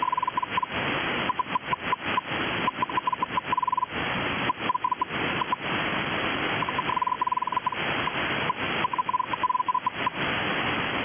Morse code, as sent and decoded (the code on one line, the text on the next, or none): none